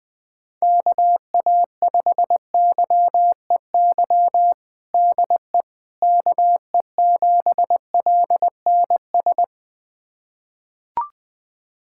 Morse code, heard as a keyed tone, about 20 wpm, 700 Hz. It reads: KA5YEY DE KE7LNS
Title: USA Callsign DE Callsign 20wpm